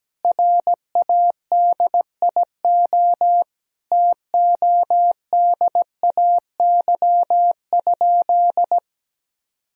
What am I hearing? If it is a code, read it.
Morse code, heard as a keyed tone, about 17 wpm, 700 Hz. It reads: RADIO TODAY?
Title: Sentences from Top 1000 Words - Review Part 3 - 17wpm